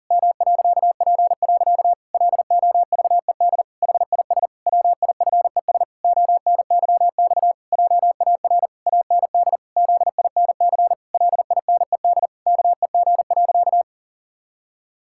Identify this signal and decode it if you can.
Morse code, heard as a keyed tone, about 40 wpm, 700 Hz. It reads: M.P. LOVED HIS WIFES ONYX JAR AND ZINC LINED KEG.
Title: Pangram Challenge v2 - 40wpm